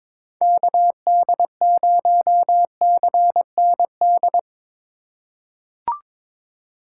Morse code, heard as a keyed tone, about 22 wpm, 700 Hz. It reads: KD0CND
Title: USA Call Signs - 6 Characters - 3x Character Spacing - 22wpm